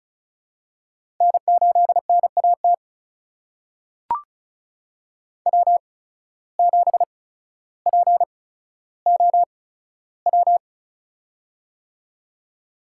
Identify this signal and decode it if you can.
Morse code, heard as a keyed tone, about 36 wpm, spaced as for 15 wpm, 700 Hz. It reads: N8NAT W7POW
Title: USA Call Signs - 8x Character Spacing - 35wpm